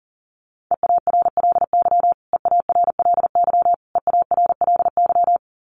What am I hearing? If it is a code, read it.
Morse code, heard as a keyed tone, about 40 wpm, 700 Hz. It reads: EARLY EARLY EARLY
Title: Top 500 Words Repeated 40wpm